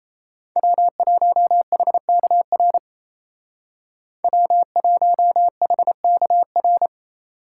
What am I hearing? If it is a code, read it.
Morse code, heard as a keyed tone, about 37 wpm, 700 Hz. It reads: W1HKR W1HKR